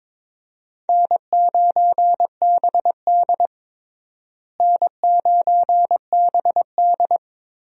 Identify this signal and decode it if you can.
Morse code, heard as a keyed tone, about 22 wpm, 700 Hz. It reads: N9BD N9BD
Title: USA Call Signs - 4 Characters - Repeated 2x - 22wpm